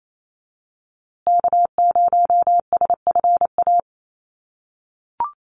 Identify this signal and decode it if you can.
Morse code, heard as a keyed tone, about 28 wpm, 700 Hz. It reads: K0SFA
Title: USA Call Signs Repeated 2x 28wpm